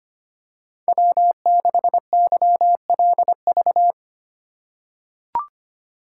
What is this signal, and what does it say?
Morse code, heard as a keyed tone, about 25 wpm, 700 Hz. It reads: W6YLV
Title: USA Call Signs - 5 Characters - 2x Character Spacing - 25wpm